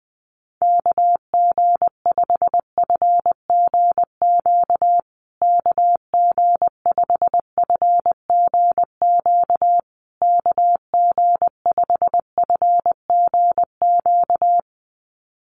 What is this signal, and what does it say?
Morse code, heard as a keyed tone, about 20 wpm, 700 Hz. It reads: KG5FGQ KG5FGQ KG5FGQ
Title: USA Call Signs Repeated 3x 20wpm